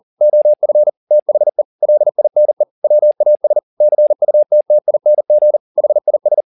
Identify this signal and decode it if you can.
Morse code, heard as a keyed tone, about 40 wpm, 600 Hz. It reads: OF THE LINE WAS CUTTING HIS
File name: the old man and the sea 2 40wpm